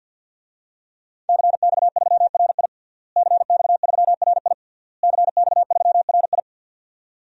Morse code, heard as a keyed tone, about 52 wpm, 700 Hz. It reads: CX3RI CX3RI CX3RI
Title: DXCC-100 Call Signs - 5 Characters - Repeated 3x - 50wpm